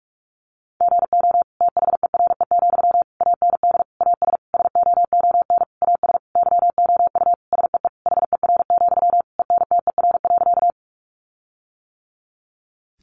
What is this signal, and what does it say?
Morse code, heard as a keyed tone, about 44 wpm, 700 Hz. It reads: GO THERE, AND AS SOON AS YOU SEE HER, ENTER.
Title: Sentences from Top 1300 Words 45wpm